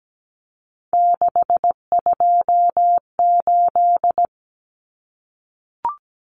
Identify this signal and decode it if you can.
Morse code, heard as a keyed tone, about 17 wpm, 700 Hz. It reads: 628
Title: Numbers - 3-Digit 17wpm